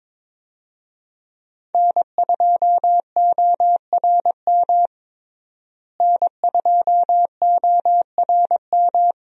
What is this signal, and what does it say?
Morse code, heard as a keyed tone, about 21 wpm, 700 Hz. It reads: N2ORM N2ORM
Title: USA Call Signs - 5 Characters - Repeated 3x - 22wpm